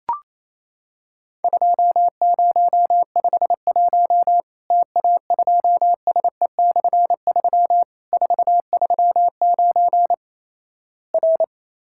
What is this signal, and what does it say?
Morse code, heard as a keyed tone, about 28 wpm, 700 Hz. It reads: 2051 TA2SE/3 439 R
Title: Worked All Europe DX Contest - QTC Exchange - 28wpm